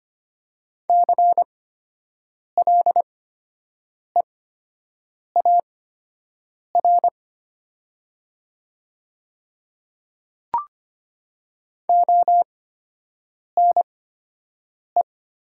Morse code, heard as a keyed tone, about 25 wpm, spaced as for 7 wpm, 700 Hz. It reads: CLEAR ONE
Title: 8x Character Spacing - Top 500 Words - 25wpm